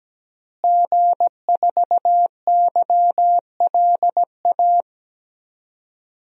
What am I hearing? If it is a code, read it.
Morse code, heard as a keyed tone, about 17 wpm, 700 Hz. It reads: G4YLA
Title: DXCC-100 Call Signs - 5 Characters - Repeated 3x - 17wpm